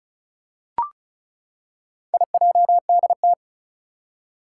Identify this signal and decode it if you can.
Morse code, heard as a keyed tone, about 36 wpm, 700 Hz. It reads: IJDT
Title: Four Challenging Letter-Number Combinations 35wpm